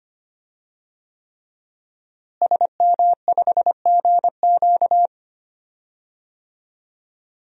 Morse code, heard as a keyed tone, about 25 wpm, 700 Hz. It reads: SM5GQ
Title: DXCC-100 Call Signs - 5 Characters - Repeated 3x - 25wpm